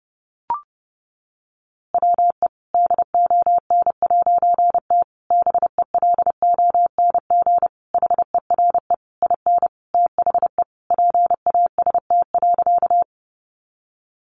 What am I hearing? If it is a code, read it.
Morse code, heard as a keyed tone, about 30 wpm, 700 Hz. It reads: WE DON'T BELONG HERE IN THE PAST.
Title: A Sound Of Thunder 30wpm